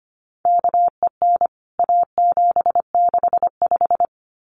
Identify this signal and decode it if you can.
Morse code, heard as a keyed tone, about 25 wpm, 700 Hz. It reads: KEN A765
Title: CWT Exchange 25wpm